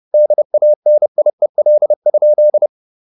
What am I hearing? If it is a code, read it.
Morse code, heard as a keyed tone, about 30 wpm, 600 Hz. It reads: DANIEL?
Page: Ferie 30 wpm